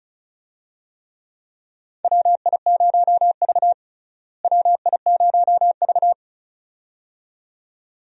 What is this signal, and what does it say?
Morse code, heard as a keyed tone, about 36 wpm, 700 Hz. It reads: WI0V WI0V
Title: USA Call Signs - 4 Characters - Repeated 2x - 35wpm